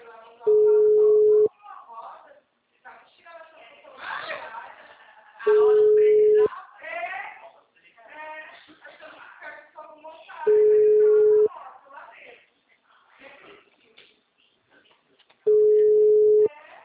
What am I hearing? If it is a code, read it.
Ringback tone (Europe, 425 Hz), with speech.